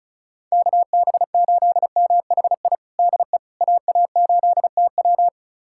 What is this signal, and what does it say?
Morse code, heard as a keyed tone, about 36 wpm, 700 Hz. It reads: KB8MHI DE AA8TW
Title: USA Callsign DE Callsign 35wpm